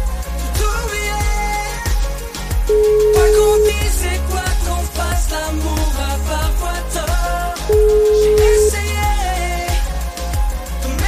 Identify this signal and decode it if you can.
Ringback tone (Europe, 425 Hz), with noise.